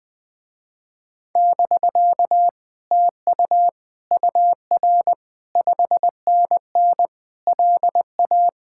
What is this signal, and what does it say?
Morse code, heard as a keyed tone, about 21 wpm, 700 Hz. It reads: <BK> TU UR 5NN LA
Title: Speed-Racing Common QSO Phrases - 20wpm